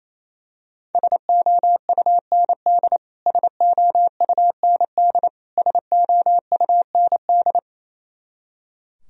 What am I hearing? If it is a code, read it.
Morse code, heard as a keyed tone, about 28 wpm, 700 Hz. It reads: SOUND SOUND SOUND
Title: Top 100 Words Repeated 28wpm